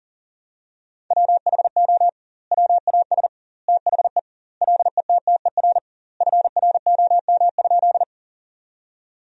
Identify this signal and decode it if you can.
Morse code, heard as a keyed tone, about 40 wpm, 700 Hz. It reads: WHO WAS THE LETTER FROM?
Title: 2x Word-Spacing - Sentences from Top 300 Words - 40wpm